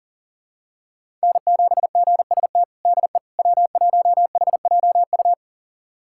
Morse code, heard as a keyed tone, about 37 wpm, 700 Hz. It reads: N7GST DE W1HJU
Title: USA Callsign DE Callsign - 5 Characters - 40wpm